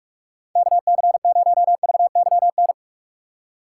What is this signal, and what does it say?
Morse code, heard as a keyed tone, about 45 wpm, 700 Hz. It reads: KC0VYN